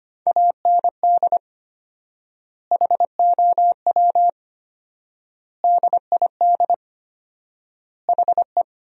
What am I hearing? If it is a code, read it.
Morse code, heard as a keyed tone, about 25 wpm, 700 Hz. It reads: AND HOW DID HE
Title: 4x Word-Spacing - Sentences from Top 400 Words - 25wpm